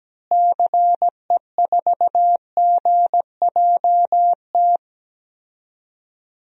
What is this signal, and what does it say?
Morse code, heard as a keyed tone, about 17 wpm, 700 Hz. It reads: CE4GJT
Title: DXCC-100 Call Signs - 6 Characters - Repeated 3x - 17wpm